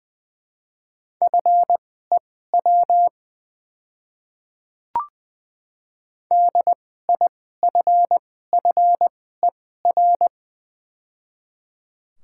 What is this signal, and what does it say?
Morse code, heard as a keyed tone, about 20 wpm, spaced as for 14 wpm, 700 Hz. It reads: FEW DIFFER
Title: 2x Character Spacing - Top 500 Words - 20wpm